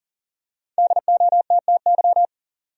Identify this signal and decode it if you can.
Morse code, heard as a keyed tone, about 40 wpm, 700 Hz. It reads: DOTTY
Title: USA Common Names - Max 5 Letters - 40wpm